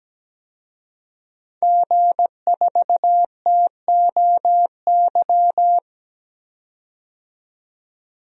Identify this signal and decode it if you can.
Morse code, heard as a keyed tone, about 17 wpm, 700 Hz. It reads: G4TOY
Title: DXCC-100 Call Signs - 5 Characters - 17wpm